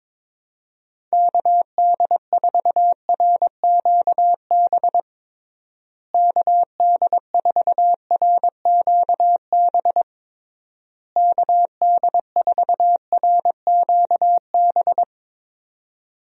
Morse code, heard as a keyed tone, about 22 wpm, 700 Hz. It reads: KD4RQB KD4RQB KD4RQB
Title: USA Call Signs - 6 Characters - Repeated 3x - 22wpm